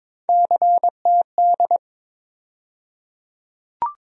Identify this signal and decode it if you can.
Morse code, heard as a keyed tone, about 22 wpm, 700 Hz. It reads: CTD
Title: USA Call Sign Postfixes - 3 Letters 22wpm